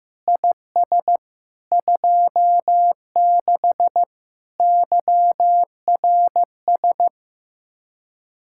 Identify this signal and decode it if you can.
Morse code, heard as a keyed tone, about 15 wpm, 700 Hz. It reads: IS 26 YRS
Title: Speed-Racing Common QSO Phrases Encore - 15wpm